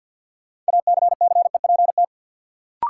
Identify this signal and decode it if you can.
Morse code, heard as a keyed tone, about 46 wpm, 700 Hz. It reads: ACCEPT
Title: Sentences from Top 1300 Words - New Word Drill Alt - 50wpm